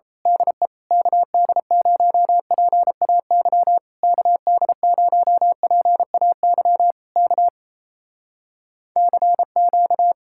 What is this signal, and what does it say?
Morse code, heard as a keyed tone, about 27 wpm, 700 Hz. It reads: DE KD0PAY KD0PAY K CQ
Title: Speed-Racing Common QSO Phrases - 28wpm